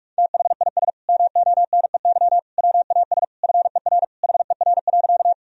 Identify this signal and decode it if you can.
Morse code, heard as a keyed tone, about 41 wpm, 700 Hz. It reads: THIS MONEY WAS FER HER.
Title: Sentences from Top 400 Words and Abbreviations 45wpm